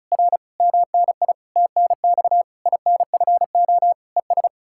Morse code, heard as a keyed tone, about 48 wpm, spaced as for 42 wpm, 700 Hz. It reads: R MNI TNX INFO ES
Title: Speed-Racing Common QSO Phrases - 35wpm